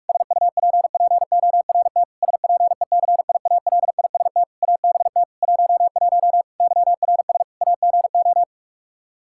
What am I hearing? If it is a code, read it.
Morse code, heard as a keyed tone, about 46 wpm, 700 Hz. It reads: SUPPORT SPECIALIST ABT 11 YRS AGO
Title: Practice QSO - Exchange 3 - More Info - 45wpm